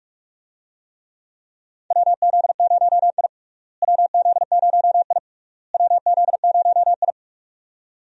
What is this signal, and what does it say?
Morse code, heard as a keyed tone, about 46 wpm, 700 Hz. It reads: WZ0I WZ0I WZ0I
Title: USA Call Signs - 4 Characters - Repeated 3x - 45wpm